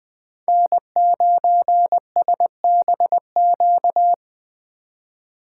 Morse code, heard as a keyed tone, about 20 wpm, 700 Hz. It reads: N9SBQ
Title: USA Call Signs - 4x Character Spacing - 20wpm